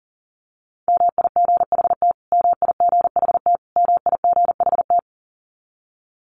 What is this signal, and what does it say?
Morse code, heard as a keyed tone, about 37 wpm, 700 Hz. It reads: MIGHT MIGHT MIGHT
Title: Top 500 Words Repeated 40wpm